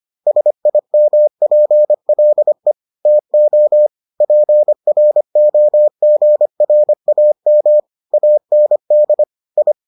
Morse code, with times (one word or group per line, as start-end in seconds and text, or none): 0.3-2.7 SIMPLE
3.1-3.9 TO
4.2-7.8 PROGRAM
8.1-9.2 AND
9.6-9.7 I